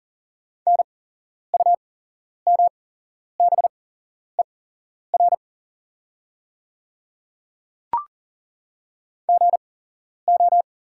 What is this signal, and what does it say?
Morse code, heard as a keyed tone, about 40 wpm, spaced as for 11 wpm, 690 Hz. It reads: NUMBER GO
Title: 8x Character Spacing - Top 100 Words - 40wpm